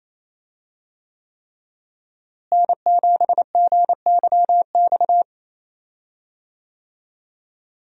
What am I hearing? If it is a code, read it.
Morse code, heard as a keyed tone, about 28 wpm, 700 Hz. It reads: N7GYX
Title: USA Call Signs - 5 Characters - Repeated 3x - 28wpm